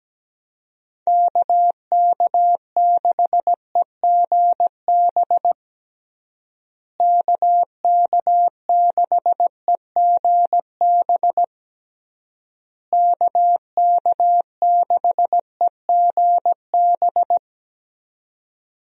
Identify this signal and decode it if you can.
Morse code, heard as a keyed tone, about 17 wpm, 700 Hz. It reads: KK6EGB KK6EGB KK6EGB
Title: USA Call Signs - 6 Characters - Repeated 3x - 17wpm